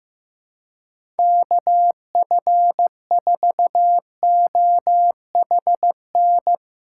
Morse code, heard as a keyed tone, about 15 wpm, 700 Hz. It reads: KF4OHN
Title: USA Callsign DE Callsign - 6 Characters - 15wpm